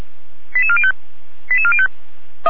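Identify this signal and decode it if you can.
Five-tone selective call (ZVEI), heard as quick runs of five tones, tones of about 70 ms. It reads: 88405 88405
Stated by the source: Five-tone ZVEI selective call received by SDR.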